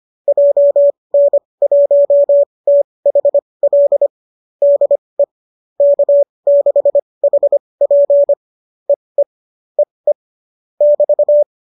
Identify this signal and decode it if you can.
Morse code, heard as a keyed tone, about 25 wpm, spaced as for 20 wpm, 570 Hz. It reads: JN1THL DE K6HP EE EE <BT>
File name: qso205 20